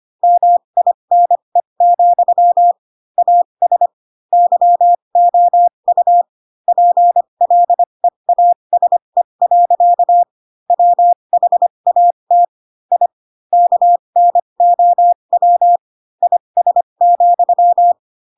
Morse code, heard as a keyed tone, about 25 wpm, spaced as for 22 wpm, 700 Hz. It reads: MINE, AS YOU PLEASE. WHAT I KNOW IS,
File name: Pinocchio / Pinocchio ch4 18 25 ik1hge